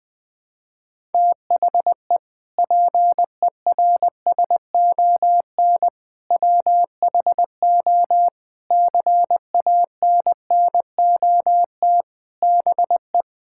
Morse code, heard as a keyed tone, about 20 wpm, 700 Hz. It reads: THE PERSON WHO CANNOT BE
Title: Sentences with 8-Word Phrases - Spoken Beginning - 20wpm